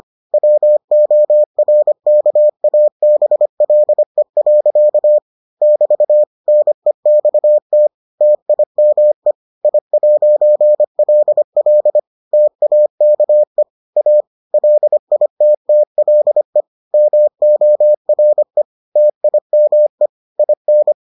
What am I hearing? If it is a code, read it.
Morse code, heard as a keyed tone, about 25 wpm, 600 Hz. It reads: WORKABLE. <BT> NEXT TIME I'LL TAKE A LITTLE MORE TIME IN